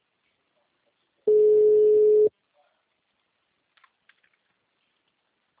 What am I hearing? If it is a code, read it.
Ringback tone (Europe, 425 Hz).
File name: out- -2004- - - .37601